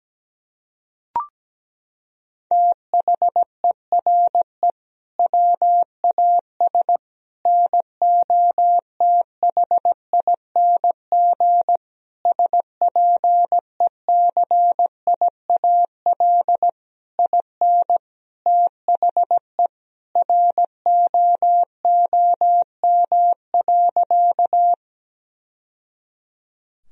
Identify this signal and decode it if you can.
Morse code, heard as a keyed tone, about 17 wpm, 700 Hz. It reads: THERE WAS NOTHING SPECIAL IN THE ROOM.
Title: Sentences from Top 1000 Words - Review Part 3 - 17wpm